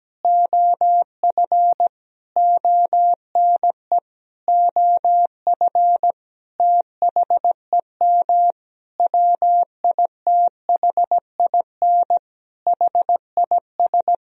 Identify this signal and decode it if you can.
Morse code, heard as a keyed tone, about 17 wpm, 700 Hz. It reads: OF ONE OF THEM WITHIN HIS
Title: Sentences from Top 1000 Words - Review Part 4 - 17wpm